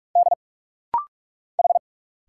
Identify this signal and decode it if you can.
Morse code, heard as a keyed tone, about 46 wpm, 690 Hz. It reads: D H
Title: Single Letters - Rapid-Fire 45wpm